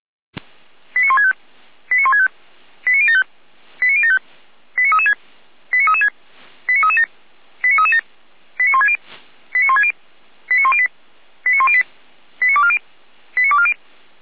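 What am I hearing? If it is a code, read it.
Five-tone selective call (ZVEI), heard as quick runs of five tones, tones of about 70 ms. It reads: 89165 89165 89075 89075 89336 89336 89337 89337 89160 89160 89108 89108 89240 89240
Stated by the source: ZVEI five-tone selective call received with an SDR.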